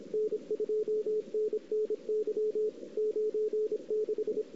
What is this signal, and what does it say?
Morse code, heard as a keyed tone, about 26 wpm, 440 Hz, with radio noise. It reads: N2NNY 96
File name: CW / 2008-ss-cw-k5tr-op-n5rz -02:00